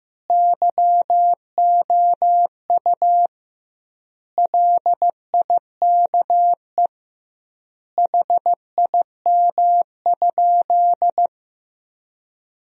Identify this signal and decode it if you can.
Morse code, heard as a keyed tone, about 15 wpm, 700 Hz. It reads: YOU LIKE HIM?
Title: 2x Word-Spacing - Sentences from Top 100 Words - 15wpm